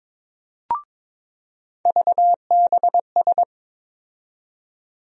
Morse code, heard as a keyed tone, about 22 wpm, 700 Hz. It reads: VBS
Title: USA Call Sign Postfixes - 3 Letters 22wpm